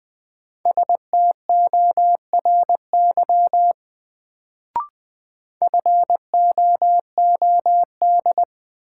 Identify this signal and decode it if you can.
Morse code, heard as a keyed tone, about 20 wpm, 700 Hz. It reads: STORY FOOD
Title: Sentences from Top 300 Words - New Word Drill - 20wpm